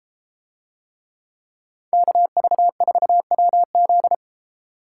Morse code, heard as a keyed tone, about 32 wpm, 700 Hz. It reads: KV4WZ